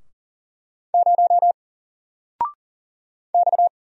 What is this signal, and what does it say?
Morse code, heard as a keyed tone, about 40 wpm, 700 Hz. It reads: 0 X
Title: Single Character (Letter:Number:Punctuation:Prosign) 40wpm